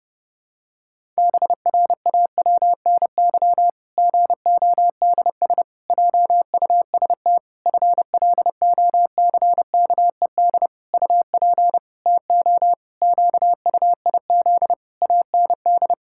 Morse code, heard as a keyed tone, about 30 wpm, 700 Hz. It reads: BRAWNY GODS JUST FLOCKED UP TO QUIZ AND
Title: Pangram Challenge v2 - 30wpm